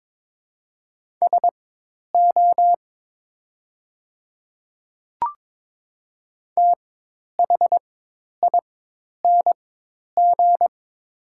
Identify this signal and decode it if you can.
Morse code, heard as a keyed tone, about 22 wpm, spaced as for 10 wpm, 700 Hz. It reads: SO THING
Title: 4x Character Spacing - Top 100 Words - 22wpm